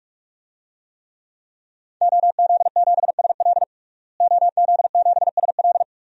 Morse code, heard as a keyed tone, about 46 wpm, 700 Hz. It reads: OZ7SL OZ7SL
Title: DXCC-100 Call Signs - 5 Characters - Repeated 2x - 45wpm